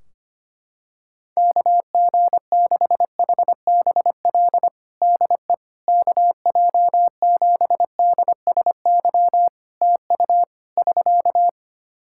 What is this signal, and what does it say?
Morse code, heard as a keyed tone, about 25 wpm, 700 Hz. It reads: KG6HBL DE KJ7DSY TU <SK>
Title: K1USN SST QSO Practice 25wpm